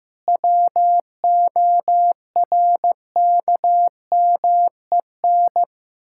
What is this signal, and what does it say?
Morse code, heard as a keyed tone, about 15 wpm, 700 Hz. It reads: WORKMEN
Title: Pangram Challenge v2 - 15wpm